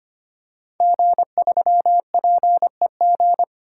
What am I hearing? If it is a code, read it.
Morse code, heard as a keyed tone, about 25 wpm, 700 Hz. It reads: G3PEG
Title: Code Groups - 5 Letter:Numbers - 25wpm